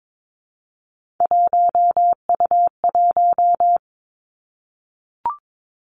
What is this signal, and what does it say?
Morse code, heard as a keyed tone, about 22 wpm, 700 Hz. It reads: 1U1
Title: Three Challenging Letter-Number Combinations 22wpm